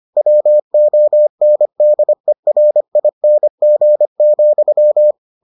Morse code, heard as a keyed tone, about 25 wpm, 600 Hz. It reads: WONDERING,